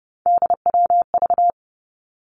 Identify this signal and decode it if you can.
Morse code, heard as a keyed tone, about 30 wpm, 700 Hz. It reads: DWV
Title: Three Challenging Letter-Number Combinations 30wpm